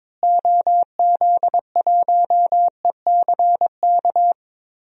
Morse code, heard as a keyed tone, about 22 wpm, 700 Hz. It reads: OZ1ECK
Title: DXCC-100 Call Signs - 6 Characters - Repeated 3x - 22wpm